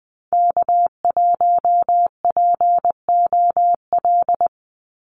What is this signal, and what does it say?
Morse code, heard as a keyed tone, about 20 wpm, 700 Hz. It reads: K1POL
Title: USA Call Signs 20wpm